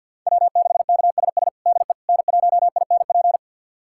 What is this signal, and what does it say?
Morse code, heard as a keyed tone, about 51 wpm, 700 Hz. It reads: W6KSS DE N1INP
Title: USA Callsign DE Callsign - 5 Characters - 50wpm